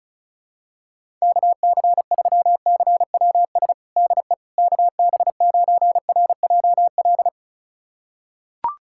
Morse code, heard as a keyed tone, about 36 wpm, 700 Hz. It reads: KC3CWS DE KB9RJL
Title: USA Callsign DE Callsign - 6 Characters - 35wpm